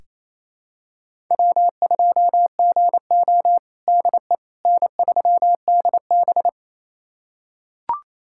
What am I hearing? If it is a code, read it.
Morse code, heard as a keyed tone, about 28 wpm, 700 Hz. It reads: W2GO DE N3DB
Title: USA Callsign DE Callsign - 4 Characters - 28wpm